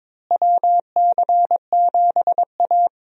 Morse code, heard as a keyed tone, about 22 wpm, 700 Hz. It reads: WC7A
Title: USA Call Signs - 4 Characters - Repeated 2x - 22wpm